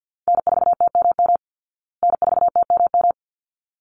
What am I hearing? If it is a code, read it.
Morse code, heard as a keyed tone, about 50 wpm, 700 Hz. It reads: N4TMM N4TMM
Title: Call Signs Repeated 3x - 50wpm